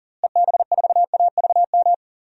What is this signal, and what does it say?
Morse code, heard as a keyed tone, about 40 wpm, 700 Hz. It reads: EB4AVM
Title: DXCC-100 Call Signs - 6 Characters - Repeated 2x - 40wpm